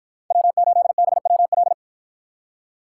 Morse code, heard as a keyed tone, about 53 wpm, 700 Hz. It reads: W8BWL